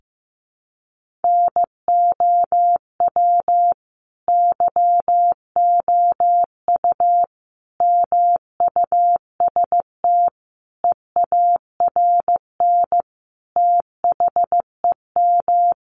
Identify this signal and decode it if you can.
Morse code, heard as a keyed tone, about 15 wpm, 700 Hz. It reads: NOW YOU MUST EARN THEM
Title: Sentences from Top 1500 Words 15wpm